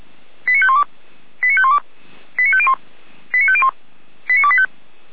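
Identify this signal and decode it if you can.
Five-tone selective call (ZVEI), heard as quick runs of five tones, tones of about 70 ms. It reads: 89512 89512 89501 89501 89285
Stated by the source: received on an SDR; ZVEI five-tone call 89338, 455 directory.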